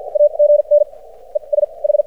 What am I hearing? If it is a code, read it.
Morse code, heard as a keyed tone, about 54 wpm, spaced as for 40 wpm, 590 Hz. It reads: AGN ES H
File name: archived / CRSnet